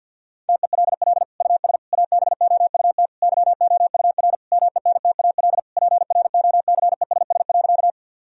Morse code, heard as a keyed tone, about 49 wpm, 700 Hz. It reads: TELL US ABOUT YOUR MENTAL PROCESS.
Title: Sentences from Top 1000 Words - Review Part 4 - 50wpm